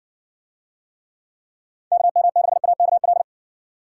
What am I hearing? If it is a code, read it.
Morse code, heard as a keyed tone, about 60 wpm, 700 Hz. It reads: KM6AKL